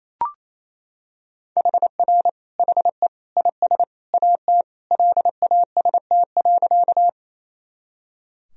Morse code, heard as a keyed tone, about 28 wpm, 700 Hz. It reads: HR HE IS AT LAST.
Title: Sentences from Top 300 Words and Abbreviations 28wpm